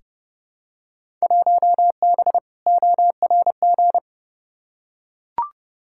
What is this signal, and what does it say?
Morse code, heard as a keyed tone, about 31 wpm, 700 Hz. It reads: 1B ORG
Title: Field Day Exchanges Encore - 30wpm